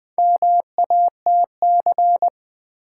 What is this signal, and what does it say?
Morse code, heard as a keyed tone, about 20 wpm, 700 Hz. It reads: MATC
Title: Build A Word - 2 Letters At A Time - 8-Letter Words - 20wpm